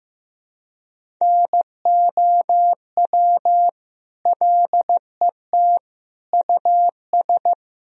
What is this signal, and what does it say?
Morse code, heard as a keyed tone, about 15 wpm, 700 Hz. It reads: NOW LET US
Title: Sentences from Top 1000 Words - Review Part 1 - 15wpm